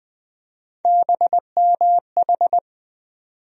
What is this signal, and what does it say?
Morse code, heard as a keyed tone, about 20 wpm, 700 Hz. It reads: BMH
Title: USA Call Sign Postfixes - 3 Letters 20wpm